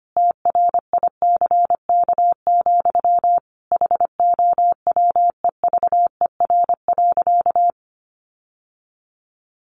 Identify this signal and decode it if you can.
Morse code, heard as a keyed tone, about 25 wpm, 700 Hz. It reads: TRICK, HOWEVER.
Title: Sentences from Top 1500 Words 25wpm